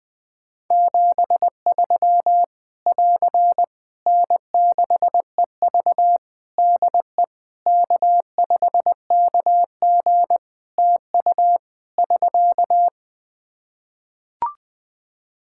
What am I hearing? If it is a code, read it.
Morse code, heard as a keyed tone, about 21 wpm, 700 Hz. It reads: 73 <AR> N6EV DE K5KG TU <SK>
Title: K1USN SST QSO Practice 20wpm